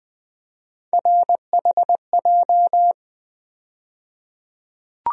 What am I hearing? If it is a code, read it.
Morse code, heard as a keyed tone, about 20 wpm, 700 Hz. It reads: RHJ
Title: USA Call Sign Postfixes - 3 Letters 20wpm